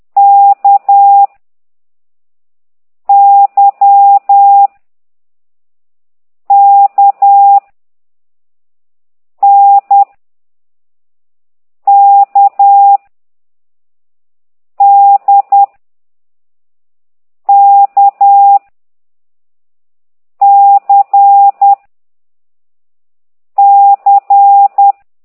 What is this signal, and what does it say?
Morse code, heard as a keyed tone, about 10 wpm, 800 Hz. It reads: K Y K N K D K C C
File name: ct2idw / exerc4